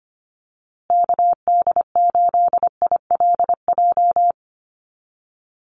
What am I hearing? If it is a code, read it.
Morse code, heard as a keyed tone, about 25 wpm, 700 Hz. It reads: KD8ILJ
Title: USA Call Signs Repeated 3x 25wpm